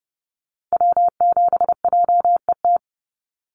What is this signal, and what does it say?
Morse code, heard as a keyed tone, about 30 wpm, 700 Hz. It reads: W7JET
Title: NA SOTA Activator Callsigns 30wpm